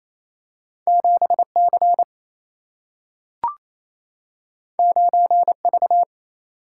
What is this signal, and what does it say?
Morse code, heard as a keyed tone, about 28 wpm, 700 Hz. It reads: 7C 9V
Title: Single-Letter Single-Number 28wpm